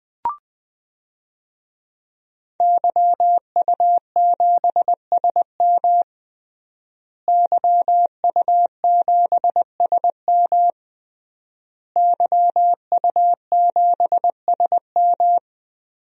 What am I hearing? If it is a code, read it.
Morse code, heard as a keyed tone, about 20 wpm, 700 Hz. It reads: YU7SM YU7SM YU7SM
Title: DXCC-100 Call Signs - 5 Characters - Repeated 3x - 20wpm